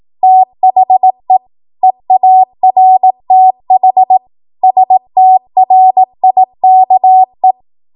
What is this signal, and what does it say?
Morse code, heard as a keyed tone, about 18 wpm, 750 Hz. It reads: THE EARTH STRIKE
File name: Archive / 18 WPM / 18WPM